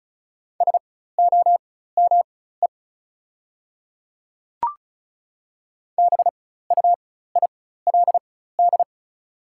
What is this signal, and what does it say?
Morse code, heard as a keyed tone, about 36 wpm, spaced as for 16 wpm, 700 Hz. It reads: SOME BUILD
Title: 4x Character Spacing - Top 300 Words - 35wpm